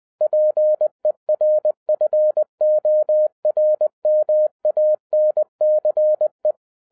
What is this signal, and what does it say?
Morse code, heard as a keyed tone, about 20 wpm, 610 Hz. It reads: PERFORMANCE
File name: 2011-07-mp3 / n58 BACK FROM A BIG ACCIDENT 20wpm